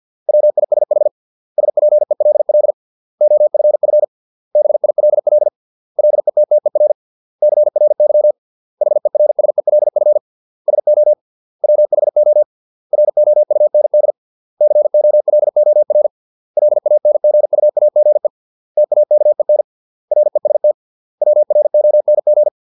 Morse code, heard as a keyed tone, about 50 wpm, 600 Hz. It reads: WISH SPELL OFF BILL LETTER CRY HERSELF SO WHO ROUND COLOR LANGUAGE TAKEN REST WRONG